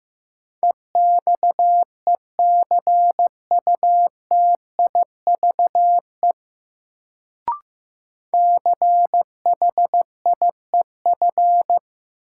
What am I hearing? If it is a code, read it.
Morse code, heard as a keyed tone, about 15 wpm, 700 Hz. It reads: EXECUTIVE CHIEF
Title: Sets of 5 Words Encore - New Word Drill - 15wpm